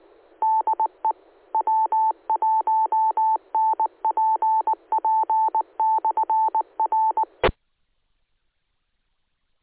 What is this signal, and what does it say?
Morse code, heard as a keyed tone, about 19 wpm, 880 Hz. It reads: DE W1NPP/R
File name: ares / AARES / aaresnet-feb-18.2016